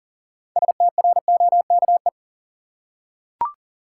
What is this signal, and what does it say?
Morse code, heard as a keyed tone, about 40 wpm, 700 Hz. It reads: STROKE
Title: Six Letter Words - 40wpm